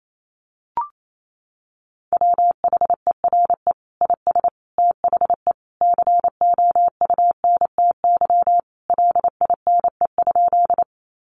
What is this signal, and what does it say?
Morse code, heard as a keyed tone, about 28 wpm, 700 Hz. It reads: WHERE IS THE COUNTY LINE?
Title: Sentences from Top 1200 Words 28wpm